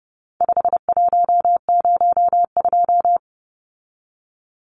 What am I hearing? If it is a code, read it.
Morse code, heard as a keyed tone, about 30 wpm, 700 Hz. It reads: 5102
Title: Numbers - 4-Digit 30wpm